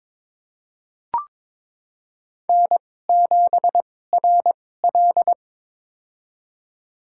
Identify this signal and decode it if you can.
Morse code, heard as a keyed tone, about 22 wpm, spaced as for 16 wpm, 700 Hz. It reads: N7RL
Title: USA Call Signs - 2x Character Spacing - 22wpm